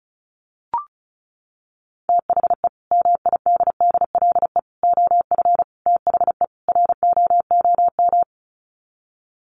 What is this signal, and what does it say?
Morse code, heard as a keyed tone, about 34 wpm, 700 Hz. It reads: THE MIDDLE OF THE ROOM
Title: Sets of 5 Words Encore2 35wpm